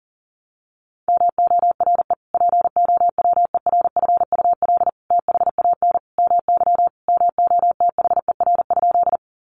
Morse code, heard as a keyed tone, about 40 wpm, 700 Hz. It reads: MORE POWERFUL THAN MY MOTHER?
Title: Sentences from Top 1300 Words 40wpm